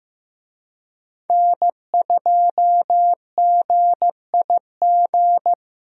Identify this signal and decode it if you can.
Morse code, heard as a keyed tone, about 15 wpm, 700 Hz. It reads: N2GIG
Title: USA Call Signs - 5 Characters - Repeated 3x - 15wpm